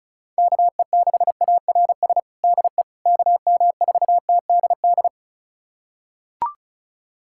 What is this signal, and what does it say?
Morse code, heard as a keyed tone, about 36 wpm, 700 Hz. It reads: KE6ARS DE KM4TDD
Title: USA Callsign DE Callsign - 6 Characters - 35wpm